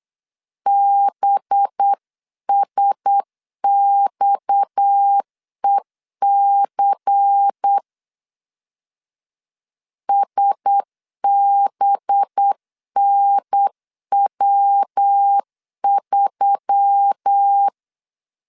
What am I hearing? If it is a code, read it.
Morse code, heard as a keyed tone, about 8 wpm, 800 Hz. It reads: BSXEC SBNW3
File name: 8wpm